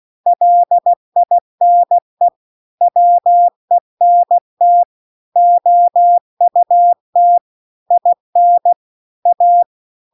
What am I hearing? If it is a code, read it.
Morse code, heard as a keyed tone, about 16 wpm, 700 Hz. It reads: LINE WENT OUT IN A
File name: CW-The-Big-Twoheaded-River-2-16WPM